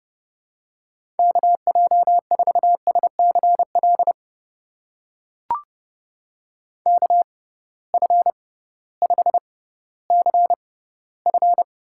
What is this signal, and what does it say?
Morse code, heard as a keyed tone, about 30 wpm, spaced as for 16 wpm, 700 Hz. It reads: KJ4SCL KF5CF
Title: USA Call Signs - 6 Characters - 6x Character Spacing - 30wpm